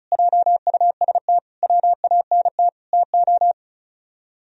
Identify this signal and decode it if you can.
Morse code, heard as a keyed tone, about 45 wpm, 700 Hz. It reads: JUST WANT TO